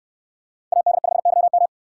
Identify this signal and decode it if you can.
Morse code, heard as a keyed tone, about 69 wpm, 700 Hz. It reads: AD5YM